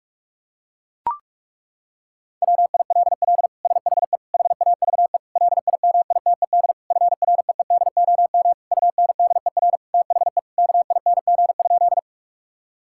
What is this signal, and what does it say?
Morse code, heard as a keyed tone, about 41 wpm, 700 Hz. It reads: WILL SHE HAVE LIMITED FREEDOM UNDER THE KING?
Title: Sentences from Top 1000 Words - Review Part 6 - 45wpm